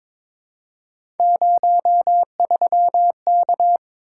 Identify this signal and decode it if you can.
Morse code, heard as a keyed tone, about 22 wpm, 700 Hz. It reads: 03K
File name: practice / 22wpm / Three Letter-Number Combinations - 22wpm